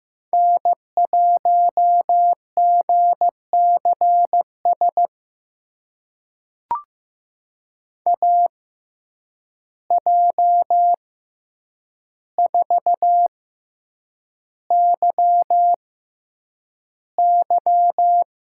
Morse code, heard as a keyed tone, about 16 wpm, spaced as for 8 wpm, 700 Hz. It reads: N1GCS AJ4YY
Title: USA Call Signs - 6x Character Spacing - 15wpm